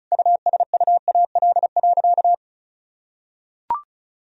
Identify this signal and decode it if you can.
Morse code, heard as a keyed tone, about 36 wpm, 700 Hz. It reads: USUAL.
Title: Sentences from Top 1000 Words - Review Part 2 - 35wpm